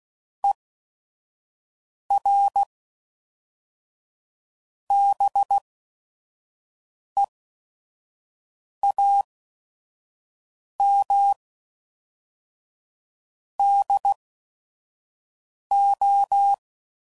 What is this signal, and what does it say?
Morse code, heard as a keyed tone, about 16 wpm, spaced as for 5 wpm, 800 Hz. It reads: ER BEAM DO
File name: CW Code Groups @5WPM / QSOs / QSO WD0FBT